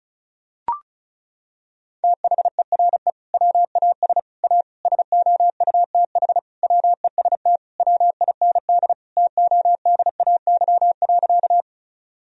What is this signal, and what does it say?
Morse code, heard as a keyed tone, about 36 wpm, 700 Hz. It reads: THERE WAS A SOUTH WEST WIND TODAY.
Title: Sentences from Top 1000 Words - Review Part 2 - 35wpm